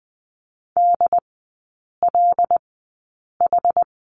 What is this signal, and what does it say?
Morse code, heard as a keyed tone, about 20 wpm, spaced as for 8 wpm, 700 Hz. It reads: DLH
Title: 100 Lesson 17e - Letter u - Three Characters 20wpm